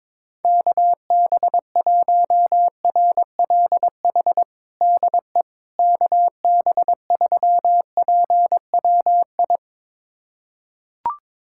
Morse code, heard as a keyed tone, about 22 wpm, 700 Hz. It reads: KB1RLH DE KB3PWI
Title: USA Callsign DE Callsign 22wpm